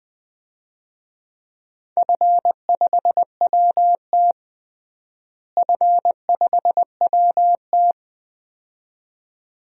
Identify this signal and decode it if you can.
Morse code, heard as a keyed tone, about 20 wpm, 700 Hz. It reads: F5WT F5WT
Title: DXCC-100 Call Signs - 4 Characters - Repeated 2x - 20wpm